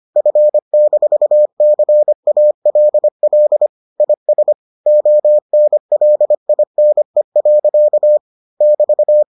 Morse code, heard as a keyed tone, about 25 wpm, 600 Hz. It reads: F-CALL IS ONLINE. <BT>